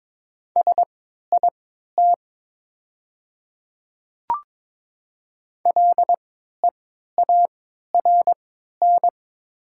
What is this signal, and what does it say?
Morse code, heard as a keyed tone, about 22 wpm, spaced as for 12 wpm, 700 Hz. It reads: SIT LEARN
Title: 3x Character Spacing - Top 500 Words - 22wpm